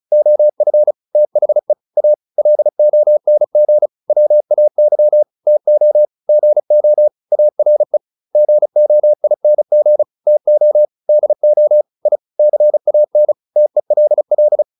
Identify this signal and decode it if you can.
Morse code, heard as a keyed tone, about 34 wpm, 600 Hz. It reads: OF THE A LONG WAY TO GO ARE GOING TO DO I CAN TELL
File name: WD402 35